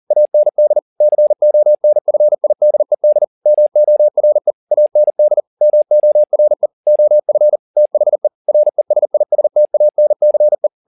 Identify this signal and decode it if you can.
Morse code, heard as a keyed tone, about 40 wpm, 600 Hz. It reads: AND CONFIDED MORE AND MORE OF THE RESISTANCE
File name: the old man and the sea 2 40wpm